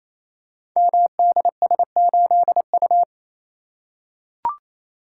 Morse code, heard as a keyed tone, about 28 wpm, 700 Hz. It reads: MDS8U
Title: Code Groups - 5 Letter:Numbers - 28wpm